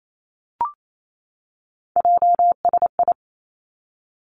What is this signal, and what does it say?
Morse code, heard as a keyed tone, about 28 wpm, 700 Hz. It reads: JSI
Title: Three Challenging Letter-Number Combinations 28wpm